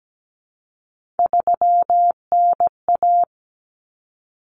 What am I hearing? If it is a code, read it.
Morse code, heard as a keyed tone, about 17 wpm, 700 Hz. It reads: 3NA
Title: Common Cut Numbers - 3-Digit 17wpm